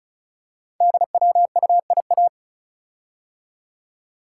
Morse code, heard as a keyed tone, about 36 wpm, 700 Hz. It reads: DWUIA
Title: Code Groups - 5 Letter:Numbers - 35wpm